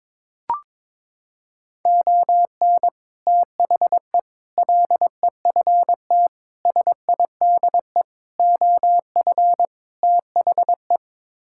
Morse code, heard as a keyed tone, about 22 wpm, 700 Hz. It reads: ON THE LEFT SIDE OF THE
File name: practice / 22wpm / Sets of 6 Words - 22wpm